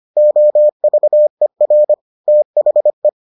Morse code, heard as a keyed tone, about 25 wpm, 600 Hz. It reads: OVER THE